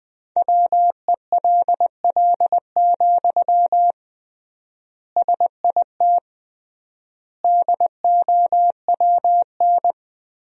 Morse code, heard as a keyed tone, about 21 wpm, 700 Hz. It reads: WELL, SIT DOWN
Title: 3x Word-Spacing - Sentences from Top 500 Words - 20wpm